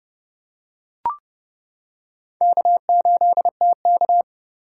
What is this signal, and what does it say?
Morse code, heard as a keyed tone, about 30 wpm, 700 Hz. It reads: K8TK